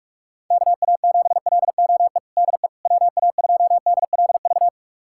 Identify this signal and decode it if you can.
Morse code, heard as a keyed tone, about 46 wpm, 700 Hz. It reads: KA7LOE DE WA2DLV
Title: USA Callsign DE Callsign - 6 Characters - 45wpm